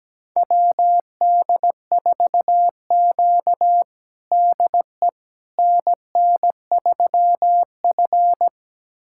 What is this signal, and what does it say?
Morse code, heard as a keyed tone, about 17 wpm, 700 Hz. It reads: WD4Q DE NN3F
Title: USA Callsign DE Callsign - 4 Characters - 17wpm